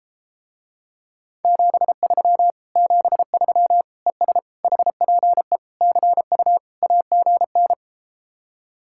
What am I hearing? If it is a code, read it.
Morse code, heard as a keyed tone, about 33 wpm, 700 Hz. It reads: 73 73 ES HPE CU AGN